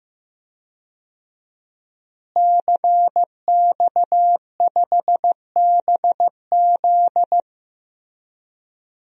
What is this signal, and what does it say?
Morse code, heard as a keyed tone, about 15 wpm, 700 Hz. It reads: CX5BZ
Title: DXCC-100 Call Signs - 5 Characters - Repeated 2x - 15wpm